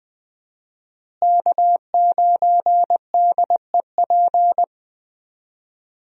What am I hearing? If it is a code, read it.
Morse code, heard as a keyed tone, about 20 wpm, 700 Hz. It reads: K9DEP
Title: USA Call Signs - 5 Characters - Repeated 3x - 20wpm